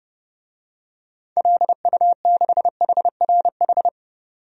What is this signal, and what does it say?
Morse code, heard as a keyed tone, about 32 wpm, 700 Hz. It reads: LU6HRH
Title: DXCC-100 Call Signs - 6 Characters - 30wpm